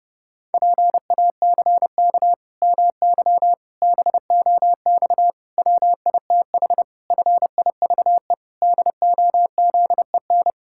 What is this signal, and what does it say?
Morse code, heard as a keyed tone, about 30 wpm, 700 Hz. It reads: PACK MY BOX WITH FIVE DOZEN
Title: Pangram Challenge v2 - 30wpm